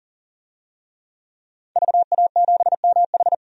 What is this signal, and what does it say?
Morse code, heard as a keyed tone, about 40 wpm, 700 Hz. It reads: VA7MH
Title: DXCC-100 Call Signs - 5 Characters - 40wpm